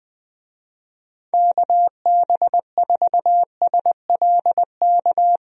Morse code, heard as a keyed tone, about 21 wpm, 700 Hz. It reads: KB4SLK
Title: USA Call Signs - 6 Characters - 20wpm